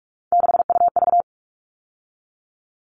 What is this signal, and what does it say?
Morse code, heard as a keyed tone, about 44 wpm, 700 Hz. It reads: 6UV
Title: All Cut Numbers - 3-Digit 45wpm